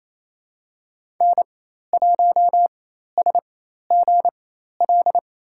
Morse code, heard as a keyed tone, about 28 wpm, spaced as for 13 wpm, 700 Hz. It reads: N1SGL
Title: USA Call Signs - 5 Characters - 4x Character Spacing - 28wpm